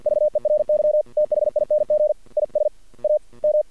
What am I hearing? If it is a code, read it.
Morse code, heard as a keyed tone, about 48 wpm, 600 Hz. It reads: VERY THING IS A M
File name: archived / CRSnet